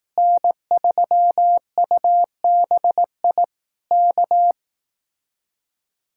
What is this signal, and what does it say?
Morse code, heard as a keyed tone, about 18 wpm, 700 Hz. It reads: N3UBI K
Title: Speed-Racing Common QSO Phrases Encore - 15wpm